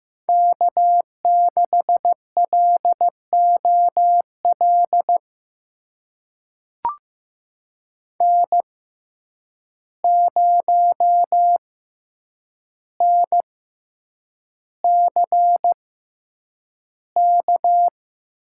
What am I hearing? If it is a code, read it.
Morse code, heard as a keyed tone, about 15 wpm, spaced as for 8 wpm, 700 Hz. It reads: K6LOL N0NCK
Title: USA Call Signs - 5 Characters - 6x Character Spacing - 15wpm